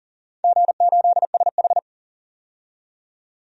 Morse code, heard as a keyed tone, about 40 wpm, 700 Hz. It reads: G8SH
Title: DXCC-100 Call Signs - 4 Characters - 40wpm